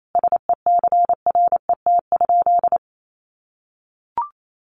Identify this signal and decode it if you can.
Morse code, heard as a keyed tone, about 28 wpm, 700 Hz. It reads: SECRET?
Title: Sentences from Top 1200 Words 28wpm